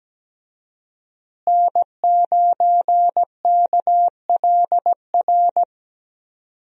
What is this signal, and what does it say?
Morse code, heard as a keyed tone, about 17 wpm, 700 Hz. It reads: N9KLR
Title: USA Call Signs - 5 Characters - Repeated 3x - 17wpm